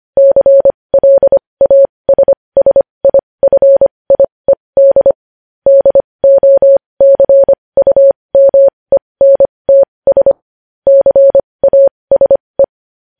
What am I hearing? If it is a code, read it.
Morse code, heard as a keyed tone, about 25 wpm, spaced as for 20 wpm, 550 Hz. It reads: CLASSIFIED DOCUMENTS CASE
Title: News Headlines at 20